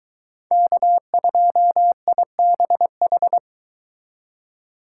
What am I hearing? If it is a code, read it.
Morse code, heard as a keyed tone, about 22 wpm, 700 Hz. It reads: K2IBH